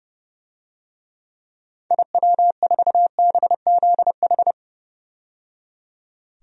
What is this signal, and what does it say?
Morse code, heard as a keyed tone, about 31 wpm, 700 Hz. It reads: IW4BZH
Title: DXCC-100 Call Signs - 6 Characters - 30wpm